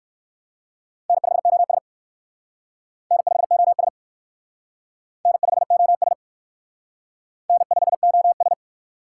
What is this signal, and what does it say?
Morse code, heard as a keyed tone, about 60 wpm, 700 Hz. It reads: N5OS N5OS N5OS N5OS